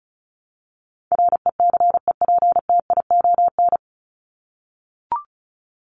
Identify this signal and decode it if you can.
Morse code, heard as a keyed tone, about 34 wpm, 700 Hz. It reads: RECEPTION
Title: Top 1751-2000 Words 35wpm